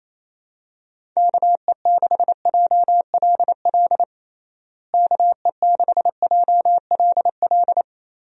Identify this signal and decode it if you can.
Morse code, heard as a keyed tone, about 28 wpm, 700 Hz. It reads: KE6JLL KE6JLL
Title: USA Call Signs - 6 Characters - Repeated 3x - 28wpm